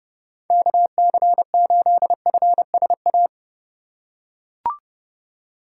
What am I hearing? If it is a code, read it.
Morse code, heard as a keyed tone, about 30 wpm, 700 Hz. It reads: KC8FSA
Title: USA Call Signs - 6 Characters - 3x Character Spacing - 30wpm